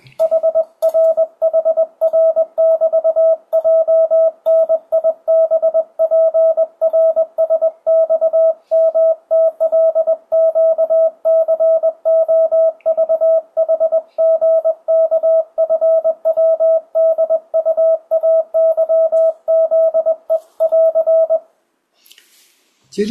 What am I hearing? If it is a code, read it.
Morse code, heard as a keyed tone, about 21 wpm, 650 Hz, with radio noise. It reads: HRHR<BT>JNIBPRSXMTLQCOVHGKFWDUAYZE<AR>
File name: 2025cw / speedup / 21wpm